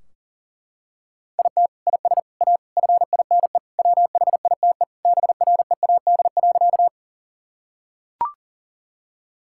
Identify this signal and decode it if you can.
Morse code, heard as a keyed tone, about 40 wpm, 700 Hz. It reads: IT IS A FINE WHITE BREAD.
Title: Sentences from Top 1600 Words 40wpm